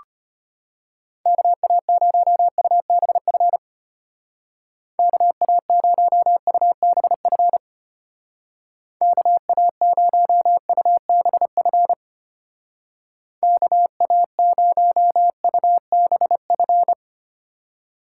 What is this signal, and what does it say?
Morse code, heard as a keyed tone, about 30 wpm, 700 Hz. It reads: KA0UBF KA0UBF KA0UBF KA0UBF
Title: Speed Racing - USA 2x3 Call Signs 25wpm